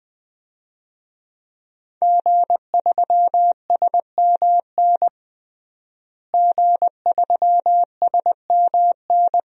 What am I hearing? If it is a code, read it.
Morse code, heard as a keyed tone, about 21 wpm, 700 Hz. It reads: G3SMN G3SMN
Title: DXCC-100 Call Signs - 5 Characters - Repeated 3x - 20wpm